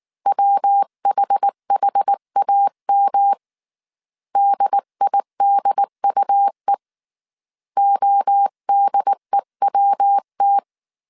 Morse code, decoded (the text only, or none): WHHAM DIDUE ODEWT